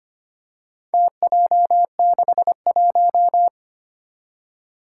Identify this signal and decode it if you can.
Morse code, heard as a keyed tone, about 25 wpm, 700 Hz. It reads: TJ61
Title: Four Challenging Letter-Number Combinations 25wpm